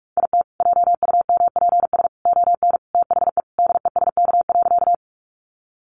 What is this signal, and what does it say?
Morse code, heard as a keyed tone, about 44 wpm, 700 Hz. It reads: IT JUMPS ON THE DESK.
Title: Additional Easy Phrases 45wpm